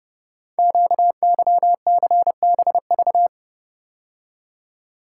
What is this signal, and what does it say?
Morse code, heard as a keyed tone, about 30 wpm, 700 Hz. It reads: QYCBV
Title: Code Groups - 5 Letters - 30wpm